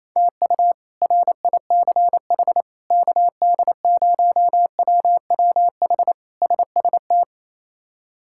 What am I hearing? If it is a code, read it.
Morse code, heard as a keyed tone, about 28 wpm, 700 Hz. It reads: TU RICH KD0WWH SST
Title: K1USN SST QSO Practice - 28wpm